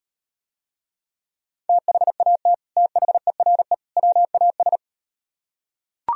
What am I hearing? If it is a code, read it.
Morse code, heard as a keyed tone, about 38 wpm, 700 Hz. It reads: THAT THERE WAS